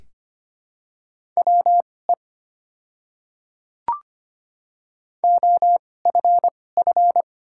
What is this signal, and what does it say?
Morse code, heard as a keyed tone, about 25 wpm, spaced as for 18 wpm, 700 Hz. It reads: WE OFF
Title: 2x Character Spacing - Top 300 Words - 25wpm